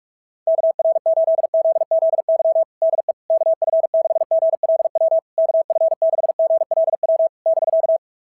Morse code, heard as a keyed tone, about 44 wpm, 660 Hz. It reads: KR8ZZY DE KF6GLW KF6GLW <BK>
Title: SOTA QSO Practice 45wpm